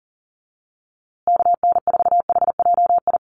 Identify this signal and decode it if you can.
Morse code, heard as a keyed tone, about 40 wpm, 700 Hz. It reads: KN4HJI
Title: USA Call Signs Repeated 2x 40wpm